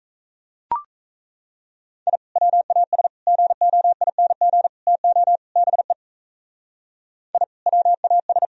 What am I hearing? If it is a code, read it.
Morse code, heard as a keyed tone, about 36 wpm, 700 Hz. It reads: I WAS GOING TO BE I WAS